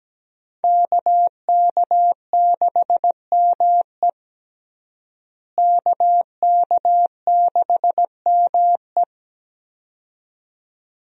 Morse code, heard as a keyed tone, about 17 wpm, 700 Hz. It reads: KK6ME KK6ME
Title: USA Call Signs - 5 Characters - Repeated 2x - 17wpm